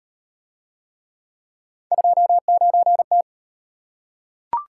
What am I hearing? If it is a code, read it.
Morse code, heard as a keyed tone, about 38 wpm, 700 Hz. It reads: 29T